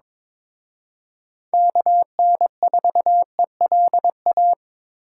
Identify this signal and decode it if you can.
Morse code, heard as a keyed tone, about 26 wpm, 700 Hz. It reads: KN4ELA
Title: Speed Racing - USA 2x3 Call Signs 22wpm